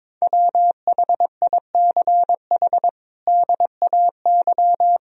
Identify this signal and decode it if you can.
Morse code, heard as a keyed tone, about 21 wpm, 700 Hz. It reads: WHICH DAY
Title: Sentences from Top 1000 Words - Review Part 3 - 22wpm